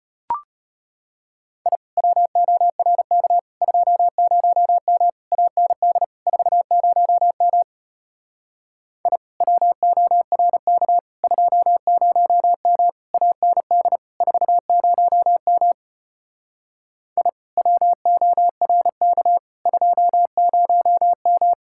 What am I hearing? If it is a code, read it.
Morse code, heard as a keyed tone, about 33 wpm, 700 Hz. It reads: I WORK 20M AND 40M I WORK 20M AND 40M I WORK 20M